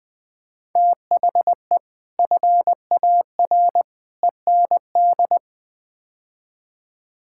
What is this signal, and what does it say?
Morse code, heard as a keyed tone, about 20 wpm, 700 Hz. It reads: THE FAR END
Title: Build A Phrase - 5-Word Phrases - 20wpm